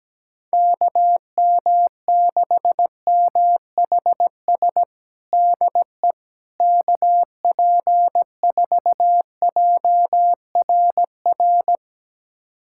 Morse code, heard as a keyed tone, about 17 wpm, 700 Hz. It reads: KM6MHS DE KP4JRR
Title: USA Callsign DE Callsign - 6 Characters - 17wpm